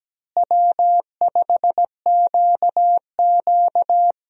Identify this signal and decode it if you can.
Morse code, heard as a keyed tone, about 17 wpm, 700 Hz. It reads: W5QQ
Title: USA Call Signs - 4 Characters - Repeated 3x - 17wpm